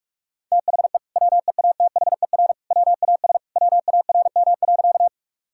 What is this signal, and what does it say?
Morse code, heard as a keyed tone, about 41 wpm, 700 Hz. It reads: THE WEATHER WAS WARM.
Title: Sentences from Top 1000 Words - Review Part 5 - 45wpm